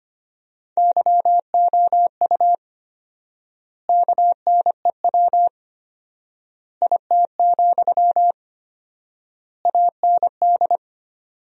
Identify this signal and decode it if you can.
Morse code, heard as a keyed tone, about 25 wpm, 700 Hz. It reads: YOU KNEW IT, AND
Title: 4x Word-Spacing - Sentences from Top 500 Words - 25wpm